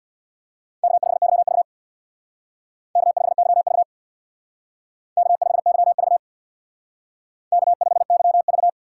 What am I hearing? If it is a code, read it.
Morse code, heard as a keyed tone, about 63 wpm, 700 Hz. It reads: K5YV K5YV K5YV K5YV